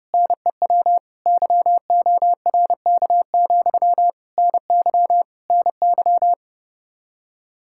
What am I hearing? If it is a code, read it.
Morse code, heard as a keyed tone, about 30 wpm, 700 Hz. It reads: NEW YORK, NY NY